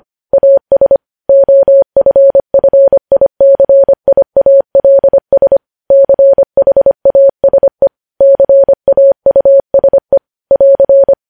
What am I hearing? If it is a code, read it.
Morse code, heard as a keyed tone, about 25 wpm, 550 Hz. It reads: AS OFFICIALS CHASE CAUSE <AR>
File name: podcast / News headlines-1x-25x25